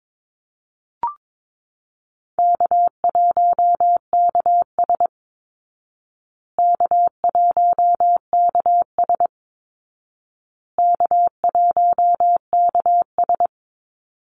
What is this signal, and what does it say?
Morse code, heard as a keyed tone, about 22 wpm, 700 Hz. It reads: K1KS K1KS K1KS
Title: Easy Call Signs Repeated 3x - 22wpm